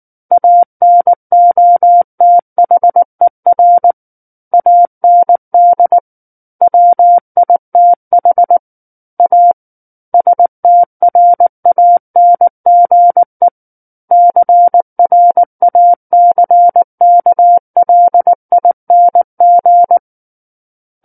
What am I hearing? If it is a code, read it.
Morse code, heard as a keyed tone, about 19 wpm, 700 Hz. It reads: ANOTHER AND WITH A STRANGE CRACKLING
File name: Book One / 10-Chapter 10 - 19 WPM